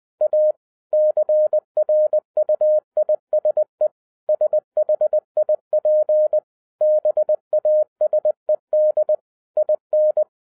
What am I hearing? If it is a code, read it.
Morse code, heard as a keyed tone, about 20 wpm, 610 Hz. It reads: A CRUISE SHIP BASED IN
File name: 2011-07-mp3 / n18 GREETINGS FROM SINGAPORE 20wpm